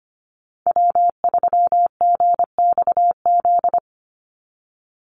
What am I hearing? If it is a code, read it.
Morse code, heard as a keyed tone, about 25 wpm, 700 Hz. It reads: W3GXZ
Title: USA Call Signs 25wpm